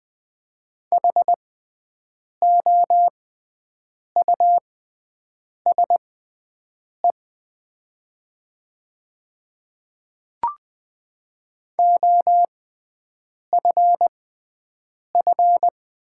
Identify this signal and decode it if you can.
Morse code, heard as a keyed tone, about 20 wpm, spaced as for 7 wpm, 700 Hz. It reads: HOUSE OFF
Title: 6x Character Spacing - Top 200 Words - 20wpm